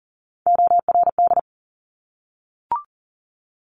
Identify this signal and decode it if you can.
Morse code, heard as a keyed tone, about 39 wpm, 700 Hz. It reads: ORD
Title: Top Three Letter Combinations in Words 40wpm